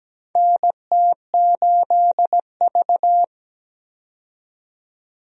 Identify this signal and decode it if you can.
Morse code, heard as a keyed tone, about 17 wpm, 700 Hz. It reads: NT8V
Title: USA Call Signs - 4 Characters - 17wpm